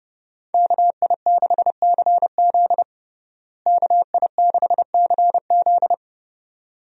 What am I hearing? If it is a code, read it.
Morse code, heard as a keyed tone, about 30 wpm, 700 Hz. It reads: KI6CZ KI6CZ
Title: USA Call Signs - 5 Characters - Repeated 2x - 30wpm